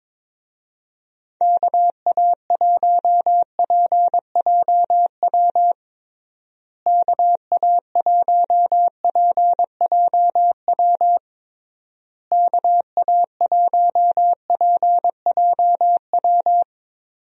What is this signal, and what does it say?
Morse code, heard as a keyed tone, about 22 wpm, 700 Hz. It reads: KA1PJW KA1PJW KA1PJW
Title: USA Call Signs - 6 Characters - Repeated 3x - 22wpm